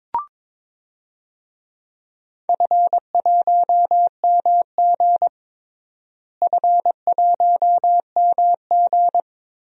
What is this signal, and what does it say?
Morse code, heard as a keyed tone, about 22 wpm, 700 Hz. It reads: F1MG F1MG
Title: DXCC-100 Call Signs - 4 Characters - Repeated 3x - 22wpm